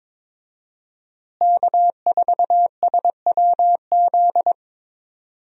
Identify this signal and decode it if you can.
Morse code, heard as a keyed tone, about 22 wpm, 700 Hz. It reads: K4SWZ
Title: USA Call Signs - 5 Characters - Repeated 2x - 22wpm